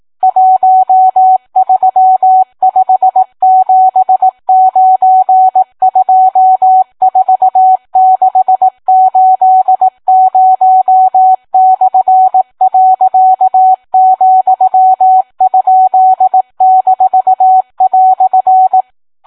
Morse code, heard as a keyed tone, about 18 wpm, 750 Hz, with radio noise. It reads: 1357924680/.,?-"